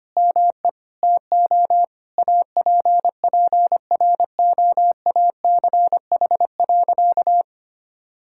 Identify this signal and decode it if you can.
Morse code, heard as a keyed tone, about 25 wpm, 700 Hz. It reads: ME TO APPROACH.
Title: Sentences from Top 1000 Words - Review Part 6 - 25wpm